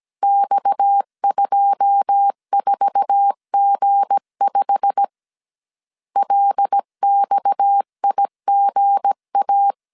Morse code, heard as a keyed tone, about 17 wpm, 800 Hz. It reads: X24G5 LXIGA
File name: CW / DET / 1 9 1500C S A 16wpm 2828